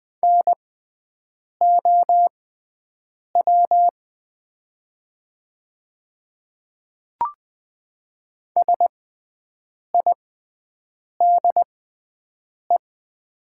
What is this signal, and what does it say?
Morse code, heard as a keyed tone, about 20 wpm, spaced as for 7 wpm, 700 Hz. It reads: NOW SIDE
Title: 6x Character Spacing - Top 100 Words - 20wpm